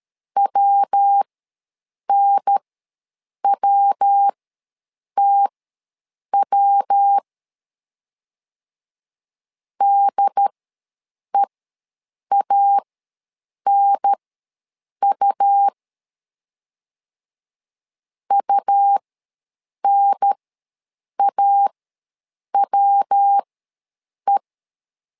Morse code, heard as a keyed tone, about 13 wpm, spaced as for 7 wpm, 800 Hz. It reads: WNWTW DEANU UNAWE
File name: LEZIONI / 3 MWRO / 3 MW ws 12wpm 1816